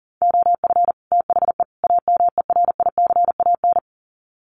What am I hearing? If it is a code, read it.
Morse code, heard as a keyed tone, about 40 wpm, 700 Hz. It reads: OF THE AMERICAN
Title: Sets of 3 Words 40wpm